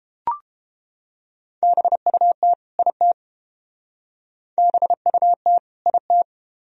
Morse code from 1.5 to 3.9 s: BUT IT

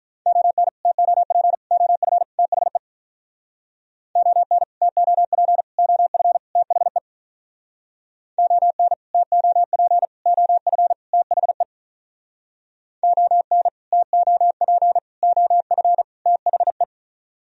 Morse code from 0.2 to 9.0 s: ON TOP OF THE ON TOP OF THE ON

